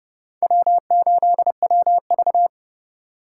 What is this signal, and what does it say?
Morse code, heard as a keyed tone, about 30 wpm, 700 Hz. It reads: W8WV
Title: USA Call Signs - 4 Characters - 2x Character Spacing - 30wpm